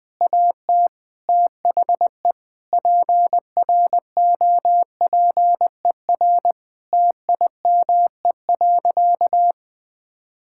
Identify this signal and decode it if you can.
Morse code, heard as a keyed tone, about 21 wpm, 700 Hz. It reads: AT THE PROPER TIME.
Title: Sentences from Top 1000 Words - Review Part 5 - 20wpm